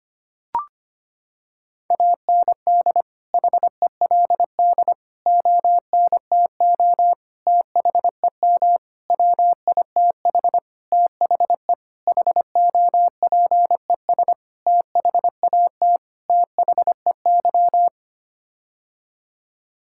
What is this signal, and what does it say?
Morse code, heard as a keyed tone, about 25 wpm, 700 Hz. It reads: AND HELD ONTO THEM WITH THE HOPES THAT THEY
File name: Sets of 9 Words - 25wpm